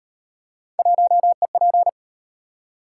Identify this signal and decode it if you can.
Morse code, heard as a keyed tone, about 38 wpm, 700 Hz. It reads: 1EP